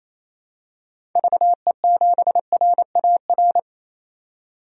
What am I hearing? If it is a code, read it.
Morse code, heard as a keyed tone, about 28 wpm, 700 Hz. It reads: VE7RAR
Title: DXCC-100 Call Signs - 6 Characters - 28wpm